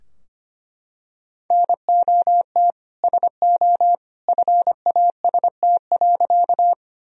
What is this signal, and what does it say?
Morse code, heard as a keyed tone, about 25 wpm, 700 Hz. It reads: NOT SO FAST.
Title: Sentences from Top 1000 Words - Review Part 2 - 25wpm